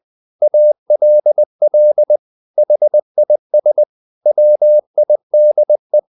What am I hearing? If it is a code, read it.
Morse code, heard as a keyed tone, about 20 wpm, 600 Hz. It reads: ALL HIS WIDE
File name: the old man and the sea / the old man and the sea part 1 20wpm